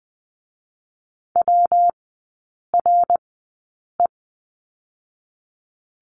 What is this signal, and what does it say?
Morse code, heard as a keyed tone, about 20 wpm, 700 Hz. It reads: W R E
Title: 114 Lesson 19e - Letter w - Three Characters 20wpm